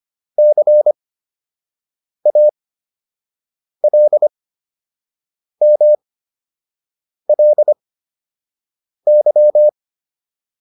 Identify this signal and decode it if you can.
Morse code, heard as a keyed tone, about 25 wpm, spaced as for 6 wpm, 600 Hz. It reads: CALMLY